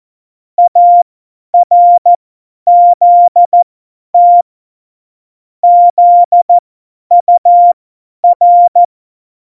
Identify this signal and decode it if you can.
Morse code, heard as a keyed tone, about 14 wpm, spaced as for 10 wpm, 700 Hz. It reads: ARZT ZUR
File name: Morsekurs / Witze / Witz 13 March 2018, 50 F70